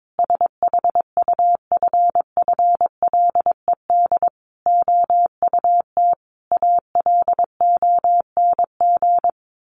Morse code, heard as a keyed tone, about 22 wpm, 700 Hz. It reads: SHUFFLED OUT ALONG
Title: A Sound Of Thunder 22wpm